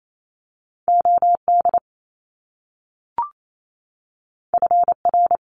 Morse code, heard as a keyed tone, about 28 wpm, 700 Hz. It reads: O D F R
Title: Top Two Letter Combinations in Words 28wpm